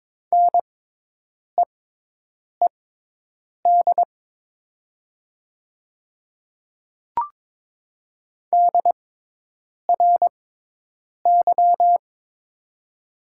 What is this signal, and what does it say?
Morse code, heard as a keyed tone, about 23 wpm, spaced as for 8 wpm, 700 Hz. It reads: NEED DRY
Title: 6x Character Spacing - Top 500 Words - 22wpm